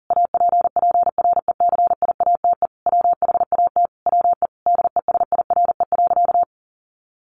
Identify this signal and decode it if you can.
Morse code, heard as a keyed tone, about 40 wpm, 700 Hz. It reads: APPRECIATE WHAT WE DESIRE.
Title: Sentences from Top 1300 Words 40wpm